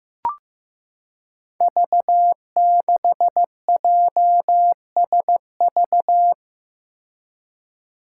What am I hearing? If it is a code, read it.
Morse code, heard as a keyed tone, about 15 wpm, 700 Hz. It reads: V6JSV
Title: Five Challenging Letter-Number Combinations 15wpm